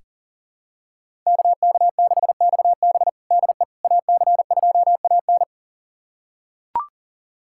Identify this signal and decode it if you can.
Morse code, heard as a keyed tone, about 40 wpm, 700 Hz. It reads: KK6XB DE AC2AN
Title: USA Callsign DE Callsign - 5 Characters - 40wpm